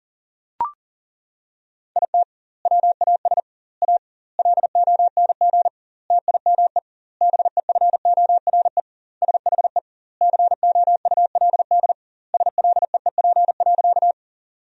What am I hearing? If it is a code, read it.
Morse code, heard as a keyed tone, about 40 wpm, 700 Hz. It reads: IT WAS A LONG TIME BEFORE SHE COULD SLEEP.
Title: 2x Word-Spacing - Sentences from Top 500 Words - 40wpm